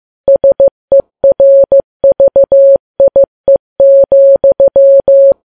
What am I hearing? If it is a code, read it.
Morse code, heard as a keyed tone, about 15 wpm, 550 Hz. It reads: SERVIE,
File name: Files Web tls / Morse on9fv / morse15kps